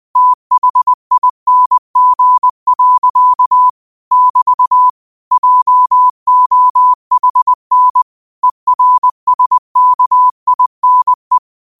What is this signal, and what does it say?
Morse code, heard as a keyed tone, about 20 wpm, 1000 Hz. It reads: THING. <BT> JOHN ERSKINE